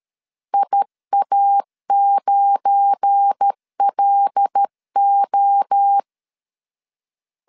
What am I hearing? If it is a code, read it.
Morse code, heard as a keyed tone, about 13 wpm, 800 Hz. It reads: IA9LO
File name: LEZIONI / 10 SIGNS / 10B INT 12wpm 2414